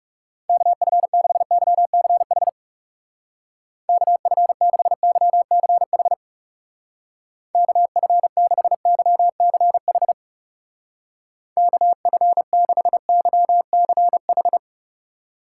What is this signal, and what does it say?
Morse code, heard as a keyed tone, about 37 wpm, 700 Hz. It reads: KF6YCH KF6YCH KF6YCH KF6YCH
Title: Speed Racing - USA Call Signs 30wpm